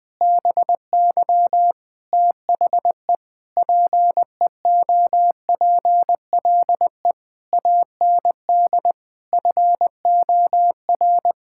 Morse code, heard as a keyed tone, about 21 wpm, 700 Hz. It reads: BY THE PEOPLE AND FOR
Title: Sentences with 9-Word Phrases - 20wpm